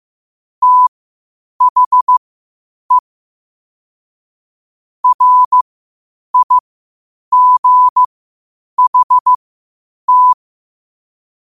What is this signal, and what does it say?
Morse code, heard as a keyed tone, about 15 wpm, spaced as for 8 wpm, 1000 Hz. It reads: THE RIGHT